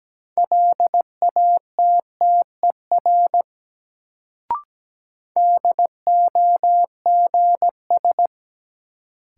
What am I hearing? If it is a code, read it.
Morse code, heard as a keyed tone, about 17 wpm, 700 Hz. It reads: LATTER DOGS
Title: Sentences from Top 1600 Words - New Word Drill Alt - 17wpm